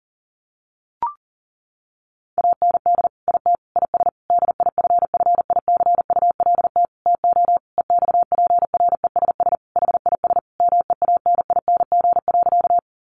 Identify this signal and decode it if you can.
Morse code, heard as a keyed tone, about 37 wpm, 700 Hz. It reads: AND IT IS DIFFICULT TO EXPRESS HIS MEANING.
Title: Sentences from Top 1100 Words 40wpm